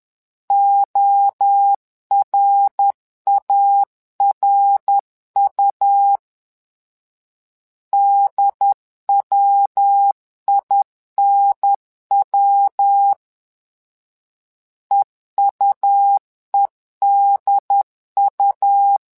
Morse code, heard as a keyed tone, about 10 wpm, 800 Hz. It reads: ORARU DWINW EUEDU
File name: LEZIONI / 3 MWRO / 3 MWRO 10wpm 2148